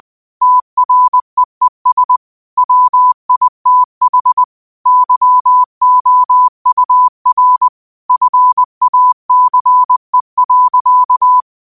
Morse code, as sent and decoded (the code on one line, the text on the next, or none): - .-. . ... / .-- .. - .... / -.-- --- ..- .-. / ..-. .- -.-. .-.-.-
TREES WITH YOUR FACE.